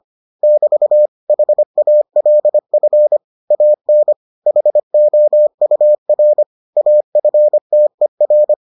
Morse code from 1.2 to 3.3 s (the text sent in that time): HALF